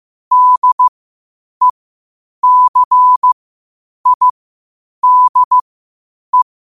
Morse code, heard as a keyed tone, about 15 wpm, spaced as for 8 wpm, 1000 Hz. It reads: DECIDE
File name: qotdfiles / 8wpm